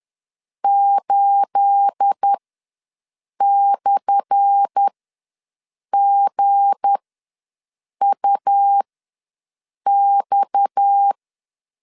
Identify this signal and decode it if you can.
Morse code, heard as a keyed tone, about 10 wpm, spaced as for 6 wpm, 800 Hz. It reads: 8/GUX
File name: LEZIONI / 10 SIGNS / 10A SLASH ws 10wpm 4121